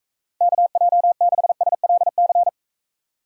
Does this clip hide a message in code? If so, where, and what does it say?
Yes: Morse code from 0.4 to 2.5 s, heard as a keyed tone, about 42 wpm, 700 Hz. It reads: KJ6SLC